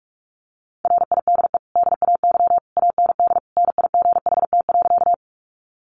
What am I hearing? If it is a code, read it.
Morse code, heard as a keyed tone, about 44 wpm, 700 Hz. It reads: RIDE DAY AND NIGHT.
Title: Sentences from Top 1200 Words 45wpm